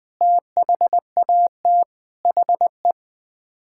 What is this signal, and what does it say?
Morse code, heard as a keyed tone, about 20 wpm, 700 Hz. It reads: THAT HE
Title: Sets of 8 Words - 20wpm